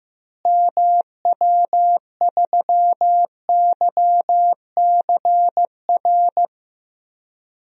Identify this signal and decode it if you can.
Morse code, heard as a keyed tone, about 15 wpm, 700 Hz. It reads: MW3YCR
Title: DXCC-100 Call Signs - 6 Characters - Repeated 3x - 15wpm